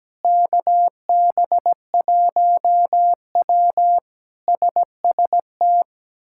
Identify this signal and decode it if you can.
Morse code, heard as a keyed tone, about 17 wpm, 700 Hz. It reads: KB1W SST
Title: K1USN SST QSO Practice - 17wpm